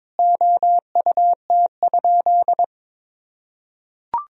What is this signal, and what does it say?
Morse code, heard as a keyed tone, about 22 wpm, 700 Hz. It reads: OUT?
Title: Sentences from Top 200 Words and Abbreviations 22wpm